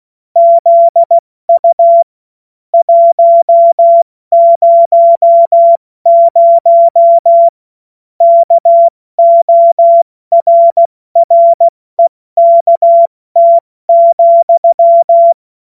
Morse code, heard as a keyed tone, about 16 wpm, 680 Hz. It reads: ZU 100 KORREKT,
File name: Morsekurs / Witze / Witz 03. Mai 2022 70 F80